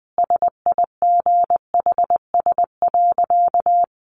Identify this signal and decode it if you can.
Morse code, heard as a keyed tone, about 20 wpm, 700 Hz. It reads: SIGHS.
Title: A Sound Of Thunder 20wpm